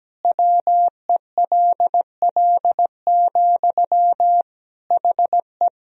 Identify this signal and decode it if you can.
Morse code, heard as a keyed tone, about 17 wpm, 700 Hz. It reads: WELL, HE
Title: Sentences from Top 1000 Words - Review Part 3 - 17wpm